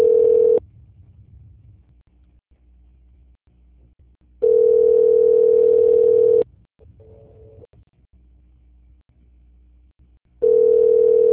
Ringback tone (North America), 440 + 480 Hz.
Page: Bangu - RJ